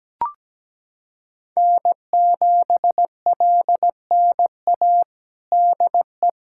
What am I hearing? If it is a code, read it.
Morse code, heard as a keyed tone, about 17 wpm, 700 Hz. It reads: N7LNA DE
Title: USA Callsign DE Callsign - 5 Characters - 17wpm